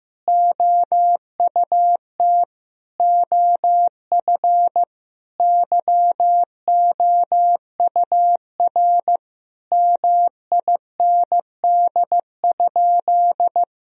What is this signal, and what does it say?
Morse code, heard as a keyed tone, about 15 wpm, 700 Hz. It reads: OUT OF YOUR MIND?
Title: Sentences from Top 1000 Words - Review Part 2 - 15wpm